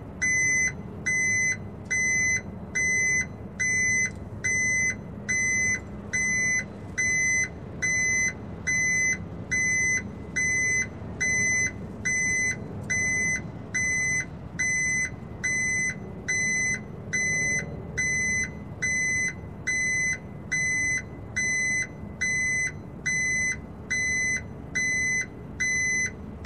There is no code.